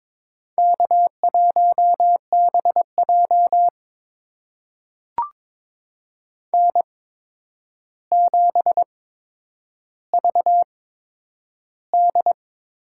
Morse code, heard as a keyed tone, about 22 wpm, spaced as for 9 wpm, 700 Hz. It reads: K1BJ N7VD
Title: USA Call Signs - 4 Characters - 8x Character Spacing - 22wpm